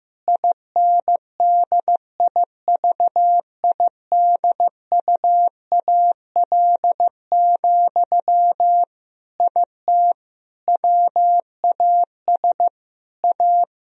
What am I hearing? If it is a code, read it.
Morse code, heard as a keyed tone, about 15 wpm, 700 Hz. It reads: INDIVIDUAL, IT WAS A
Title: Sentences from Top 1000 Words - Review Part 4 - 15wpm